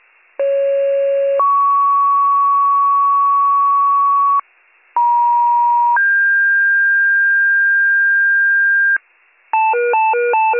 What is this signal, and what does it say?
Two-tone page, heard as a one-second tone, then a three-second tone, about 550 then 1120 Hz, about 950 then 1600 Hz, with noise. It is heard from 0.4 to 9.0 s.